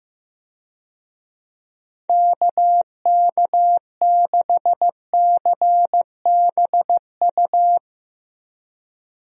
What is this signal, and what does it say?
Morse code, heard as a keyed tone, about 16 wpm, 700 Hz. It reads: KK6CBU
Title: USA Call Signs - 6 Characters - Repeated 3x - 15wpm